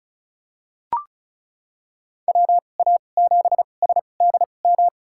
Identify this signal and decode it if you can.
Morse code, heard as a keyed tone, about 34 wpm, spaced as for 25 wpm, 700 Hz. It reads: WA7SDM
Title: USA Call Signs - 2x Character Spacing - 35wpm